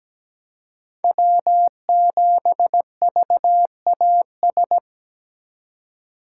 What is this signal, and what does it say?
Morse code, heard as a keyed tone, about 17 wpm, 700 Hz. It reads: W7VAS
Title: K1USN SST QSO Practice - 17wpm